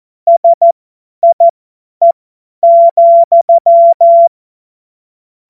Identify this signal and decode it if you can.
Morse code, heard as a keyed tone, about 14 wpm, spaced as for 10 wpm, 680 Hz. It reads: SIE,
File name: Morsekurs / Witze / Witz 12 März 2024 50 F70